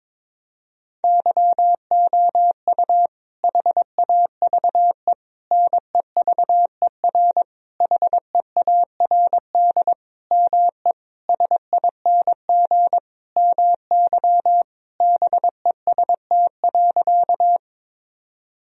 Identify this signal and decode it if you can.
Morse code, heard as a keyed tone, about 22 wpm, 700 Hz. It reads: YOU HAVE NEVER HEARD ME SING MY BEST.
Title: Sentences from Top 1000 Words - Review Part 2 - 22wpm